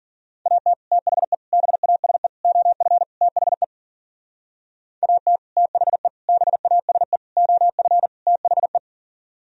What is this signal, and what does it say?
Morse code, heard as a keyed tone, about 53 wpm, 700 Hz. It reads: AT THE BASE OF THE AT THE BASE OF THE